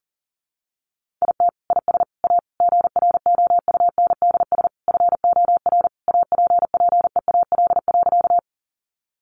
Morse code, heard as a keyed tone, about 40 wpm, 700 Hz. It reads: IT IS A GROUNDS FOR APPEAL.
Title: Sentences from Top 1300 Words 40wpm